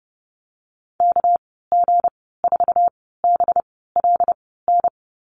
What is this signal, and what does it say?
Morse code, heard as a keyed tone, about 32 wpm, spaced as for 17 wpm, 700 Hz. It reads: KG4BLN
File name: Lesson 37g - Letter x - Call Signs 30wpm